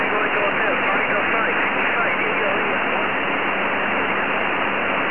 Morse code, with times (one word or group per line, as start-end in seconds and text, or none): none